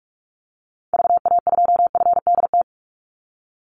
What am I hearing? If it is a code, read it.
Morse code, heard as a keyed tone, about 44 wpm, 700 Hz. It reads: VA2FDT
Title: Call Signs - 45wpm